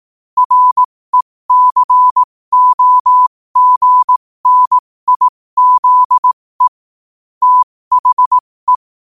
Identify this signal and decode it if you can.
Morse code, heard as a keyed tone, about 18 wpm, spaced as for 15 wpm, 1000 Hz. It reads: RECOGNIZE THE
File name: qotdfiles / 15wpm